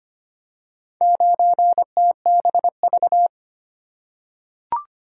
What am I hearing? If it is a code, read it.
Morse code, heard as a keyed tone, about 25 wpm, 700 Hz. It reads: 9TBV
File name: practice / 25wpm / Four Letter-Number Combinations - 25wpm